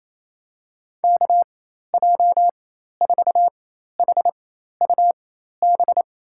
Morse code, heard as a keyed tone, about 27 wpm, spaced as for 13 wpm, 700 Hz. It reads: KJ4HUB
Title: USA Call Signs - 4x Character Spacing - 28wpm